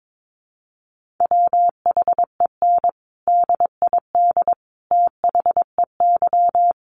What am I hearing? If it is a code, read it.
Morse code, heard as a keyed tone, about 22 wpm, 700 Hz. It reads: WHEN DID THEY
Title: Sentences from Top 900 Words 22wpm